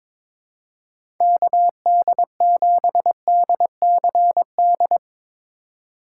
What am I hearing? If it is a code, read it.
Morse code, heard as a keyed tone, about 23 wpm, 700 Hz. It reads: KD7DCD
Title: USA Call Signs - 6 Characters - 8x Character Spacing - 22wpm